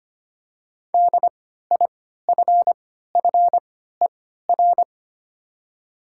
Morse code, heard as a keyed tone, about 25 wpm, spaced as for 14 wpm, 700 Hz. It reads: DIFFER
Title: 3x Character Spacing - Top 200 Words - 25wpm